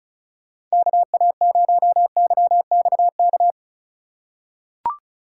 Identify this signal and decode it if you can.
Morse code, heard as a keyed tone, about 34 wpm, 700 Hz. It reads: KA0YXK
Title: USA Call Signs - 6 Characters - 2x Character Spacing - 35wpm